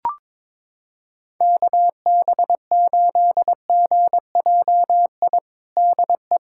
Morse code, heard as a keyed tone, about 22 wpm, 700 Hz. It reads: KB8GJI DE
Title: USA Callsign DE Callsign 22wpm